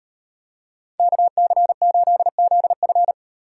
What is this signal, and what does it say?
Morse code, heard as a keyed tone, about 38 wpm, 700 Hz. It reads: KC8ZF